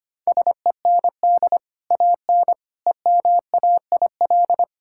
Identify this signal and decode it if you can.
Morse code, heard as a keyed tone, about 25 wpm, 700 Hz. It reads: SEND AN EMAIL
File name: Easy Sentences with Hints - 25wpm